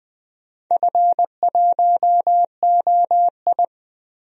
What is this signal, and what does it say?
Morse code, heard as a keyed tone, about 20 wpm, 690 Hz. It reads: F1OI
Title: DXCC-100 Call Signs - 4 Characters - Repeated 3x - 20wpm